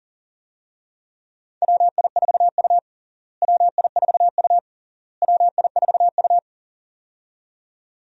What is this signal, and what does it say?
Morse code, heard as a keyed tone, about 40 wpm, 700 Hz. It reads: WI4U WI4U WI4U
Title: USA Call Signs - 4 Characters - Repeated 3x - 40wpm